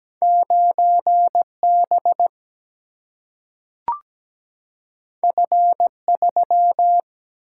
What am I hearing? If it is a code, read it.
Morse code, heard as a keyed tone, about 17 wpm, 700 Hz. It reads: 9 B F 3
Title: Single-Letter Single-Number 17wpm